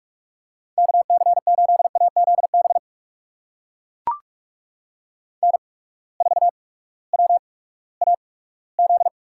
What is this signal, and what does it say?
Morse code, heard as a keyed tone, about 44 wpm, spaced as for 20 wpm, 700 Hz. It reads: KC8AZB N4WAZ
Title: USA Call Signs - 8x Character Spacing - 45wpm